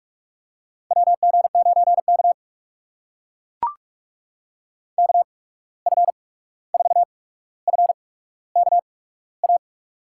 Morse code, heard as a keyed tone, about 44 wpm, spaced as for 17 wpm, 700 Hz. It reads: WG9K KF4FKA
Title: USA Call Signs - 8x Character Spacing - 45wpm